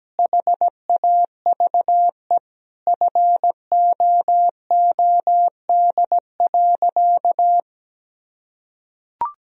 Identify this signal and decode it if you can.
Morse code, heard as a keyed tone, about 17 wpm, 700 Hz. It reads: HAVE FOOD.
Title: Sentences from Top 1000 Words - Review Part 1 - 17wpm